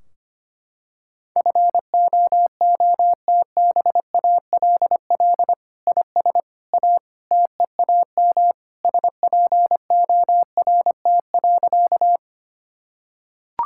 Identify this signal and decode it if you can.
Morse code, heard as a keyed tone, about 25 wpm, 700 Hz. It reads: FOOTBALL IS A TEAM SPORT.
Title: Sentences from Top 1600 Words 25wpm